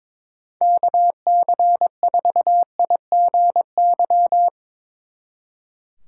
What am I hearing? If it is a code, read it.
Morse code, heard as a keyed tone, about 23 wpm, 700 Hz. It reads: KC4IGY
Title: Speed Racing - USA Call Signs 22wpm